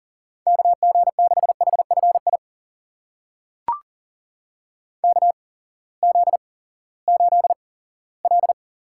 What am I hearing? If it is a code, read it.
Morse code, heard as a keyed tone, about 42 wpm, spaced as for 20 wpm, 690 Hz. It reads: KG6HFI KZ8L
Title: USA Call Signs - 8x Character Spacing - 40wpm